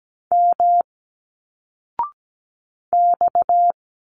Morse code, heard as a keyed tone, about 17 wpm, 700 Hz. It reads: M X
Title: Single Letters 17wpm